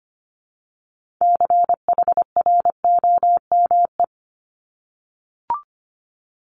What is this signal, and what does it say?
Morse code, heard as a keyed tone, about 25 wpm, 700 Hz. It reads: CHROME
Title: Multi-letter Phonemes Repeated 3x 25wpm